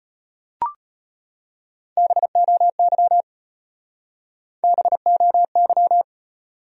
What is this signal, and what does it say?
Morse code, heard as a keyed tone, about 33 wpm, 700 Hz. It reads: BOY BOY